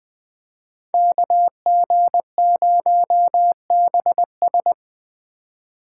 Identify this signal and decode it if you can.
Morse code, heard as a keyed tone, about 25 wpm, 700 Hz. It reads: KG0BS
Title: Speed Racing - USA 2x2 Call Signs 20wpm